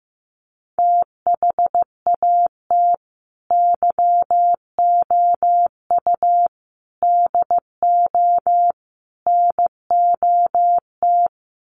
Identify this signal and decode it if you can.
Morse code, heard as a keyed tone, about 15 wpm, 700 Hz. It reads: THAT YOU DO NOT
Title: Sets of 4 Words 15wpm